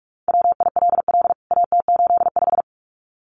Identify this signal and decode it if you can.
Morse code, heard as a keyed tone, about 44 wpm, 700 Hz. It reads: WILL AT85
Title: CWT Exchange 45wpm